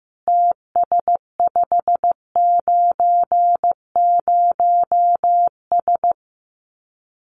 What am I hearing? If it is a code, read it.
Morse code, heard as a keyed tone, about 15 wpm, 700 Hz. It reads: TS590S
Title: Common QSO Phrases Encore 15wpm